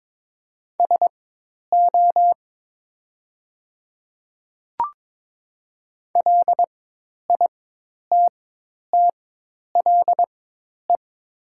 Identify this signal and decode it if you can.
Morse code, heard as a keyed tone, about 22 wpm, spaced as for 10 wpm, 700 Hz. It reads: SO LITTLE
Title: 4x Character Spacing - Top 200 Words - 22wpm